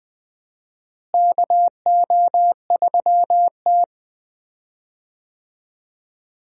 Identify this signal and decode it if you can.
Morse code, heard as a keyed tone, about 20 wpm, 700 Hz. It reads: KO3T
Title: USA Call Signs - 4 Characters - Repeated 2x - 20wpm